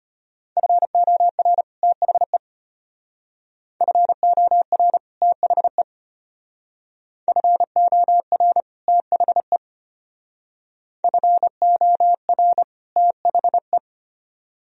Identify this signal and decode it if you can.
Morse code, heard as a keyed tone, about 31 wpm, 700 Hz. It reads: FOR THE FOR THE FOR THE FOR THE
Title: Speed-Racing - Sets of 2 Words 25wpm